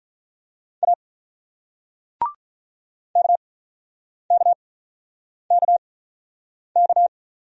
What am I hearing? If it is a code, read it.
Morse code, heard as a keyed tone, about 48 wpm, spaced as for 9 wpm, 700 Hz. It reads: A KKKK